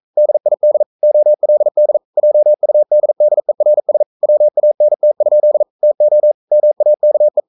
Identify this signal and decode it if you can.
Morse code, heard as a keyed tone, about 42 wpm, 600 Hz. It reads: DID OLD JUDDERS WANT? TO MAKE
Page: Romanen Never af Ken Follet på engelsk.